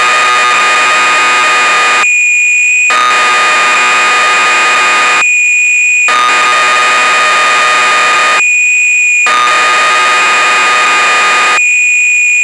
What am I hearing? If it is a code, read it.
Cassette data (Kansas City type).
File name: Acorn User/1986-04 BBC Programs